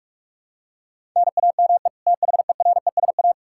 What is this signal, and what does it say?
Morse code, heard as a keyed tone, about 44 wpm, 700 Hz. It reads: NAME THERESA
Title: K1USN SST QSO Practice 45wpm